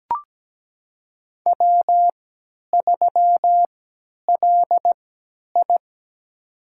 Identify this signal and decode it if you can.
Morse code, heard as a keyed tone, about 17 wpm, spaced as for 10 wpm, 700 Hz. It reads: W3LI
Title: USA Call Signs - 3x Character Spacing - 17wpm